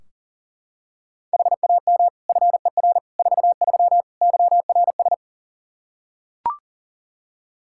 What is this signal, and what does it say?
Morse code, heard as a keyed tone, about 39 wpm, 700 Hz. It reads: HAM FER 43 YRS
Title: Practice QSO - Exchange 3 - More Info - 40wpm